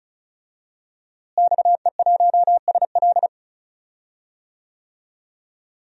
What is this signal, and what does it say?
Morse code, heard as a keyed tone, about 36 wpm, 700 Hz. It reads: XE1SL
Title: DXCC-100 Call Signs - 5 Characters - Repeated 3x - 35wpm